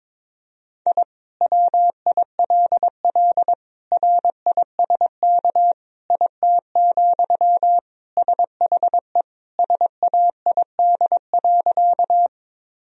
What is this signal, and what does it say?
Morse code, heard as a keyed tone, about 22 wpm, 700 Hz. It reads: I WILL RISK IT, SHE SAID.
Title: Sentences from Top 1000 Words - Review Part 4 - 22wpm